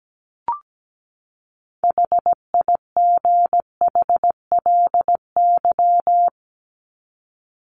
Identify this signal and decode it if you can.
Morse code, heard as a keyed tone, about 17 wpm, 700 Hz. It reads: HIGHLY
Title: LY Suffix - Top 100 Words with Base Words 17wpm